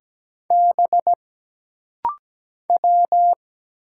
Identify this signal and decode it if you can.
Morse code, heard as a keyed tone, about 17 wpm, 700 Hz. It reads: B W
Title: Single Letters - Rapid-Fire 17wpm